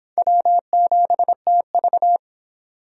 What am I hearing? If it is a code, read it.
Morse code, heard as a keyed tone, about 31 wpm, 700 Hz. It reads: W7TV